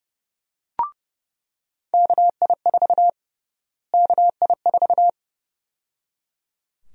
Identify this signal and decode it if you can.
Morse code, heard as a keyed tone, about 30 wpm, 700 Hz. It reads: KI4 KI4
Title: USA Call Sign Prefixes - Repeated 2x - 30wpm